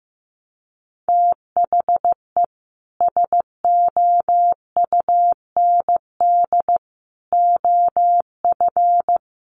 Morse code, heard as a keyed tone, about 15 wpm, 700 Hz. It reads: THE SOUND OF
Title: Sets of 3 Words 15wpm